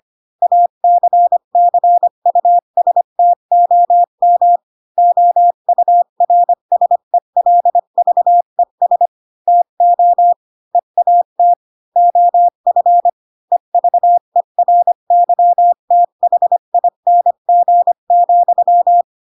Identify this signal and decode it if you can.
Morse code, heard as a keyed tone, about 25 wpm, 700 Hz. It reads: ACCUSTOM OURSELVES TO EAT OF EVERYTHING,
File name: Pinocchio / Pinocchio ch7 20 25 ik1hge